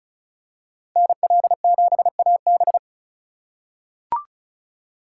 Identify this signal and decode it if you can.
Morse code, heard as a keyed tone, about 34 wpm, 700 Hz. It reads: NL7AB
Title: USA Call Signs - 5 Characters - 3x Character Spacing - 35wpm